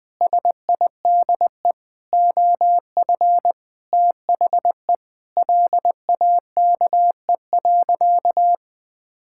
Morse code, heard as a keyed tone, about 19 wpm, 700 Hz. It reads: SIDE OF THE LAKE.
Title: Sentences with 5-Word Phrases - 20wpm